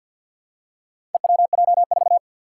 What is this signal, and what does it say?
Morse code, heard as a keyed tone, about 46 wpm, 700 Hz. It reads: EWJ4
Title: Four Challenging Letter-Number Combinations 50wpm